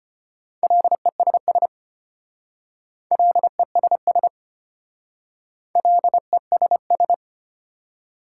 Morse code, heard as a keyed tone, about 31 wpm, 700 Hz. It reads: LESS LESS LESS
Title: Speed-Racing Prefixes and Suffixes 25wpm